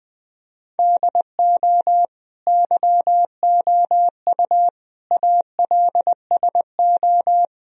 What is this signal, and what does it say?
Morse code, heard as a keyed tone, about 20 wpm, 700 Hz. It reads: DO YOU ALSO
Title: Sentences from Top 1000 Words - Review Part 1 - 20wpm